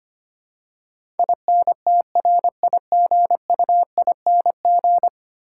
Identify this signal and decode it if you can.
Morse code, heard as a keyed tone, about 25 wpm, 700 Hz. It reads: INTRIGUING
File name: 10 Letter Words - First 3 Letters Given - 25wpm